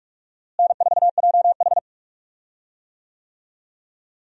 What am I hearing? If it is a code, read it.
Morse code, heard as a keyed tone, about 46 wpm, 700 Hz. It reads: N4JH
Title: USA Call Signs - 4 Characters - Repeated 3x - 45wpm